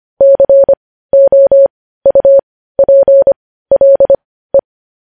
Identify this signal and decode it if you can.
Morse code, heard as a keyed tone, about 25 wpm, spaced as for 15 wpm, 550 Hz. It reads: COUPLE
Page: news headlines in morse code at 15 wpm